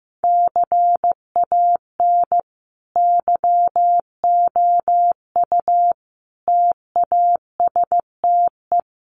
Morse code, heard as a keyed tone, about 15 wpm, 700 Hz. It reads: CAN YOU TASTE
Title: Sentences from Top 1200 Words 15wpm